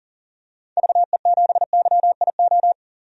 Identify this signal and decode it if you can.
Morse code, heard as a keyed tone, about 40 wpm, 700 Hz. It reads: VE7YIO
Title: DXCC-100 Call Signs - 6 Characters - 40wpm